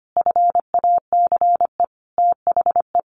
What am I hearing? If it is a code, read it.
Morse code, heard as a keyed tone, about 25 wpm, 700 Hz. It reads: FACE THE
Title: Sentences from Top 1200 Words 25wpm